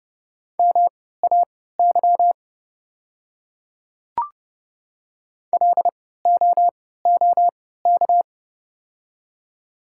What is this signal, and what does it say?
Morse code, heard as a keyed tone, about 30 wpm, spaced as for 17 wpm, 700 Hz. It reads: MAY LOOK
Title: 3x Character Spacing - Top 100 Words - 30wpm